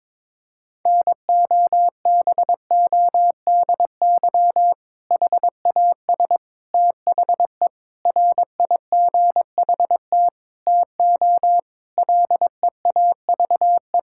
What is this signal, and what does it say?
Morse code, heard as a keyed tone, about 22 wpm, 700 Hz. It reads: NOBODY HAS THE RIGHT TO LEAVE
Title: Sentences from Top 1600 Words 22wpm